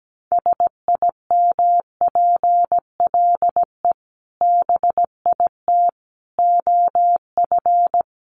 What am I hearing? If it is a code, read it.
Morse code, heard as a keyed tone, about 17 wpm, 700 Hz. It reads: SIMPLE BIT OF
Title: Sentences from Top 900 Words 17wpm